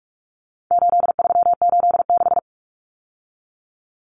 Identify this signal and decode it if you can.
Morse code, heard as a keyed tone, about 44 wpm, 700 Hz. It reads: 8386
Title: Numbers - 4-Digit 45wpm